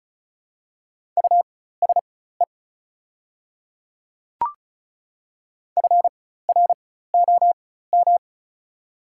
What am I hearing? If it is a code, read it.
Morse code, heard as a keyed tone, about 34 wpm, spaced as for 16 wpm, 700 Hz. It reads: USE FROM
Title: 4x Character Spacing - Top 200 Words - 35wpm